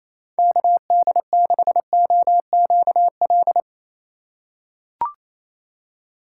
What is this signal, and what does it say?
Morse code, heard as a keyed tone, about 27 wpm, 700 Hz. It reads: KD6OQL
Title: USA Call Signs - 6x Character Spacing - 28wpm